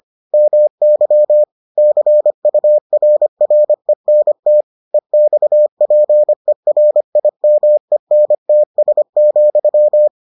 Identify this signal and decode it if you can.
Morse code, heard as a keyed tone, about 25 wpm, 600 Hz. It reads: MY CURRENT EXPERIMENTS,